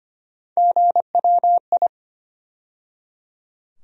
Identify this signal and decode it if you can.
Morse code, heard as a keyed tone, about 25 wpm, 700 Hz. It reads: GWI
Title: USA Call Sign Postfixes - 3 Letters 25wpm